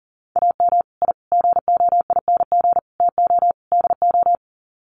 Morse code, heard as a keyed tone, about 40 wpm, 700 Hz. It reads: AM I GOING TO DO
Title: Sets of 5 Words Encore 40wpm